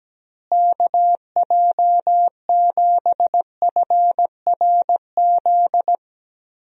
Morse code, heard as a keyed tone, about 17 wpm, 700 Hz. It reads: KJ7FRZ
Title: USA Call Signs - 6 Characters - Repeated 3x - 17wpm